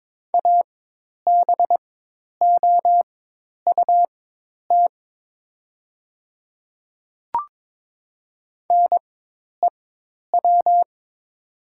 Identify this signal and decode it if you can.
Morse code, heard as a keyed tone, about 23 wpm, spaced as for 10 wpm, 700 Hz. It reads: ABOUT NEW
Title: 4x Character Spacing - Top 300 Words - 22wpm